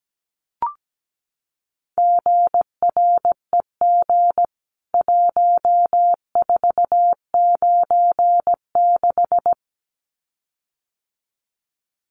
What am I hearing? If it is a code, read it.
Morse code, heard as a keyed tone, about 17 wpm, 700 Hz. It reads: GREG 1496
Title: CWT Exchange 17wpm